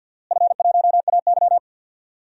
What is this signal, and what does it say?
Morse code, heard as a keyed tone, about 52 wpm, 700 Hz. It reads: F1AY
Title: DXCC-100 Call Signs - 4 Characters - Repeated 3x - 50wpm